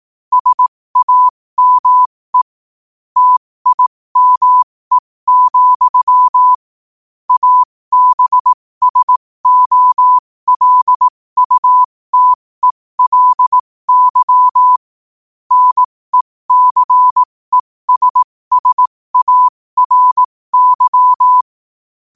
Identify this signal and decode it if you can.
Morse code, heard as a keyed tone, about 18 wpm, spaced as for 15 wpm, 1000 Hz. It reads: SAME TIME, ABSOLUTELY NECESSARY
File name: qotdfiles / 15wpm